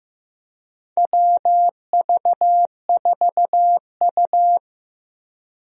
Morse code, heard as a keyed tone, about 15 wpm, 700 Hz. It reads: WV4U
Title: USA Call Signs - 4 Characters - Repeated 2x - 15wpm